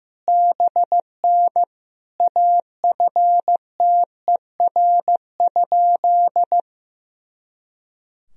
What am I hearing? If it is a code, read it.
Morse code, heard as a keyed tone, about 15 wpm, 700 Hz. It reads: BN AFTER?
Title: Sentences from Top 200 Words and Abbreviations 15wpm